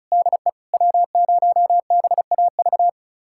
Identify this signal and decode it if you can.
Morse code, heard as a keyed tone, about 36 wpm, 700 Hz. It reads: DE W0BAV
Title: USA Callsign DE Callsign - 5 Characters - 35wpm